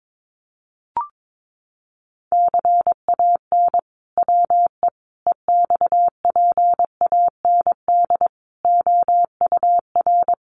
Morse code, heard as a keyed tone, about 22 wpm, 700 Hz. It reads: CAN WE EXPAND OUR
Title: Sentences from Top 1400 Words 22wpm